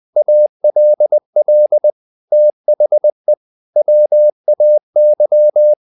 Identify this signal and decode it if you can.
Morse code, heard as a keyed tone, about 20 wpm, 600 Hz. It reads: ALL THE WAY
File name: the old man and the sea part 2 20wpm